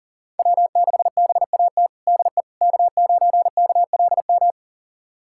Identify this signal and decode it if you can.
Morse code, heard as a keyed tone, about 40 wpm, 700 Hz. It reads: W6BAT DE K9KLM
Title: USA Callsign DE Callsign - 5 Characters - 40wpm